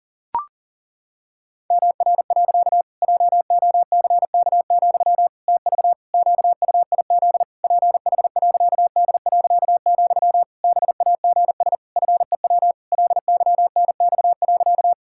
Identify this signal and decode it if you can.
Morse code, heard as a keyed tone, about 40 wpm, 700 Hz. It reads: MR. JOCK, TV QUIZ PH.D., BAGS FEW LYNX.
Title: Pangram Challenge v2 - 40wpm